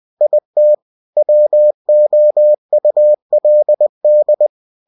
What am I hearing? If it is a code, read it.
Morse code, heard as a keyed tone, about 20 wpm, 600 Hz. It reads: IT WOULD
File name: cw-books / winnie-the-pooh / winnie-20wpm-page-0103